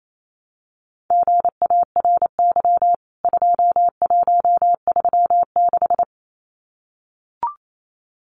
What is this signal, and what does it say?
Morse code, heard as a keyed tone, about 28 wpm, 700 Hz. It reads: GARY 2136
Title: CWT Exchange 28wpm